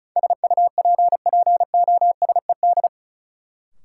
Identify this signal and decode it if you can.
Morse code, heard as a keyed tone, about 36 wpm, 700 Hz. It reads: SUPPOSED
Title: Sets of 3 Words - New Word Drill - 35wpm